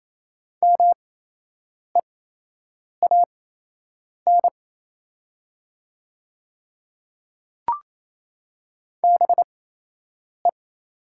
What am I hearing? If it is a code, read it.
Morse code, heard as a keyed tone, about 28 wpm, spaced as for 8 wpm, 700 Hz. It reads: MEAN BE
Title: 8x Character Spacing - Top 200 Words - 28wpm